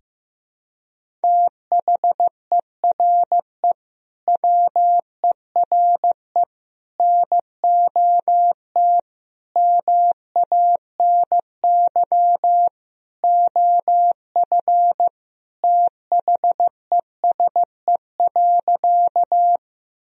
Morse code, heard as a keyed tone, about 15 wpm, 700 Hz. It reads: THERE WERE NOT MANY OF THESE.
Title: Sentences from Top 1000 Words - Review Part 1 - 15wpm